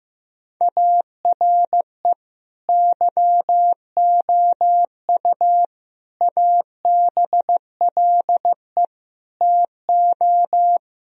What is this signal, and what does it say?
Morse code, heard as a keyed tone, about 15 wpm, 700 Hz. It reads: ARE YOU ABLE TO
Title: Sentences from Top 1000 Words - Review Part 3 - 15wpm